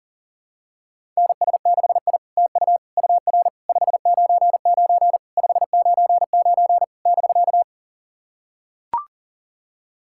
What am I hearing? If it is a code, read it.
Morse code, heard as a keyed tone, about 40 wpm, 700 Hz. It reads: NS6I TU UR 599 599 <BK>
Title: SOTA QSO Practice 40wpm